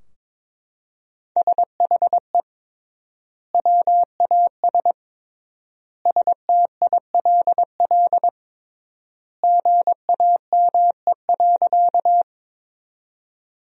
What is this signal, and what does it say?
Morse code, heard as a keyed tone, about 21 wpm, 700 Hz. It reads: SHE WAS STILL GAME.
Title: 3x Word-Spacing - Sentences from Top 500 Words - 22wpm